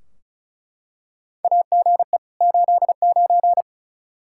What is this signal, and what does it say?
Morse code, heard as a keyed tone, about 34 wpm, 700 Hz. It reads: AGE 89
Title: Practice QSO - Exchange 3 - More Info - 35wpm